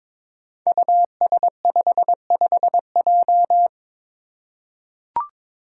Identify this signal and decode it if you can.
Morse code, heard as a keyed tone, about 22 wpm, 700 Hz. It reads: US55J
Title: Five Challenging Letter-Number Combinations 22wpm